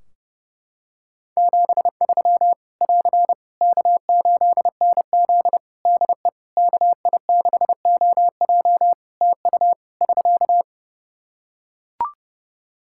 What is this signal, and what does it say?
Morse code, heard as a keyed tone, about 30 wpm, 700 Hz. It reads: 73 <AR> K8NZ DE KI6OJ TU <SK>
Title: Practice QSO - Exchange 4 - Ending - 30wpm